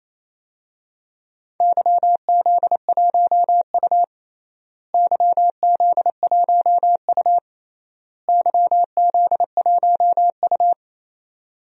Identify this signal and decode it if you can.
Morse code, heard as a keyed tone, about 28 wpm, 700 Hz. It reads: YZ1U YZ1U YZ1U
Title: DXCC-100 Call Signs - 4 Characters - Repeated 3x - 28wpm